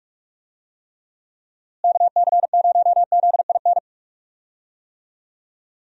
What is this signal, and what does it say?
Morse code, heard as a keyed tone, about 41 wpm, 700 Hz. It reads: KC0ZIN
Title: USA Call Signs - 6 Characters - Repeated 3x - 45wpm